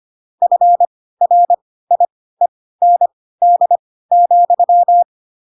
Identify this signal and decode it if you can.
Morse code, heard as a keyed tone, about 25 wpm, spaced as for 16 wpm, 700 Hz. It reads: FRIEND,
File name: Pinocchio / Pinocchio ch2 10 25 ik1hge